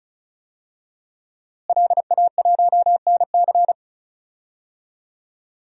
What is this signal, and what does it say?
Morse code, heard as a keyed tone, about 36 wpm, 700 Hz. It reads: LA1NC
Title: DXCC-100 Call Signs - 5 Characters - Repeated 2x - 35wpm